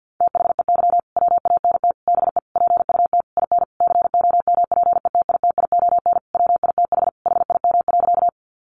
Morse code, heard as a keyed tone, about 49 wpm, 700 Hz. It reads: THEY WANT BE PUT IN COMPETITION WITH HIM.
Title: Sentences from Top 1000 Words 50wpm